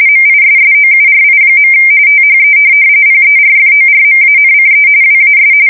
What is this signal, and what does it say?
RTTY, about 45.45 baud. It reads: MISSOURI. THE PERFORMANCE WAS
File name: fsk170 / USS-Missouri